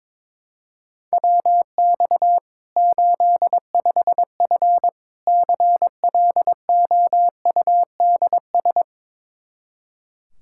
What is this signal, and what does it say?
Morse code, heard as a keyed tone, about 22 wpm, 700 Hz. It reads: WX 85F CLOUDS
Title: Practice QSO - Exchange 2 - Info - 22wpm